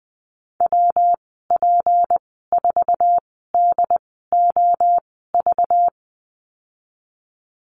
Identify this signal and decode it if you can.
Morse code, heard as a keyed tone, about 20 wpm, spaced as for 14 wpm, 700 Hz. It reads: WP4DOV
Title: 193 Lesson 31g - Letter v - Call Signs 20wpm